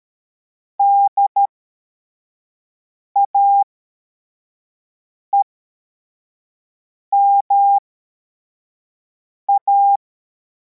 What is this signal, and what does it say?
Morse code, heard as a keyed tone, about 13 wpm, spaced as for 4 wpm, 800 Hz. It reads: DAEMA
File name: formation / cw / Text23